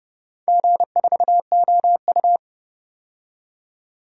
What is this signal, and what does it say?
Morse code, heard as a keyed tone, about 30 wpm, 700 Hz. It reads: G4OU
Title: DXCC-100 Call Signs - 4 Characters - 30wpm